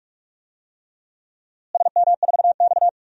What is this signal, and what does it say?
Morse code, heard as a keyed tone, about 42 wpm, 700 Hz. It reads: SM4X
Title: DXCC-100 Call Signs - 4 Characters - Repeated 3x - 45wpm